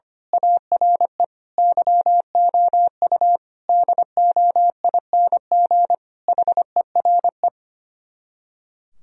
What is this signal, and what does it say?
Morse code, heard as a keyed tone, about 24 wpm, 700 Hz. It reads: ARE YOU DOING HERE
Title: Speed-Racing - Sets of 5 Words 25wpm